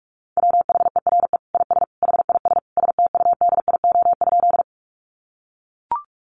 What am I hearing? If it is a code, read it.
Morse code, heard as a keyed tone, about 44 wpm, 700 Hz. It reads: WHERE IS HIS STUDIO?
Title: Sentences from Top 1200 Words 45wpm